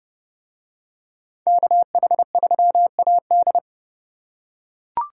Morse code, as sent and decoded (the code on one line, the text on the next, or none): -.- .... ...-- .- -..
KH3AD